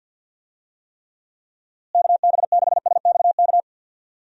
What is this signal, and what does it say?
Morse code, heard as a keyed tone, about 46 wpm, 700 Hz. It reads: KB6SXK
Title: USA Call Signs - 6 Characters - Repeated 2x - 50wpm